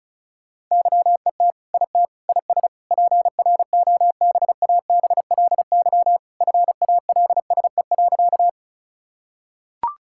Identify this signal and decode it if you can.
Morse code, heard as a keyed tone, about 34 wpm, 700 Hz. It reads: YET IT IS PROBABLY FALSE.
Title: Sentences from Top 1600 Words 35wpm